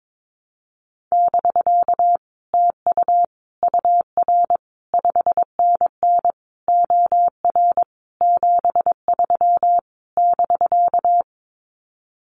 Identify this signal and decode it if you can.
Morse code, heard as a keyed tone, about 22 wpm, 700 Hz. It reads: <BK> TU UR 5NN OR 73 <BK>
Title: Common QSO Phrases Encore 22wpm